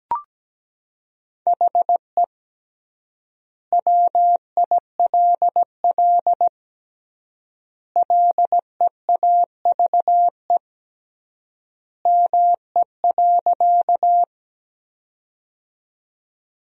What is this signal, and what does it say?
Morse code, heard as a keyed tone, about 17 wpm, 700 Hz. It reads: HE WILL LEAVE ME.
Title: 3x Word-Spacing - Sentences from Top 400 Words - 17wpm